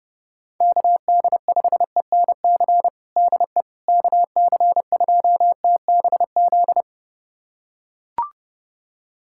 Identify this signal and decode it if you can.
Morse code, heard as a keyed tone, about 32 wpm, 700 Hz. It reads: KD5ENC DE KC2TBZ
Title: USA Callsign DE Callsign - 6 Characters - 30wpm